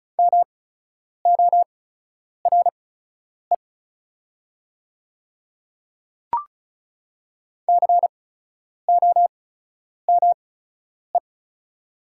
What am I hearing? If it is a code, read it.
Morse code, heard as a keyed tone, about 36 wpm, spaced as for 10 wpm, 700 Hz. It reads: MORE COME
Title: 8x Character Spacing - Top 100 Words - 35wpm